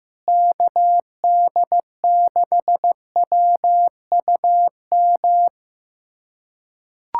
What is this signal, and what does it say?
Morse code, heard as a keyed tone, about 15 wpm, 700 Hz. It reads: KD6WUM
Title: USA Callsign DE Callsign - 6 Characters - 15wpm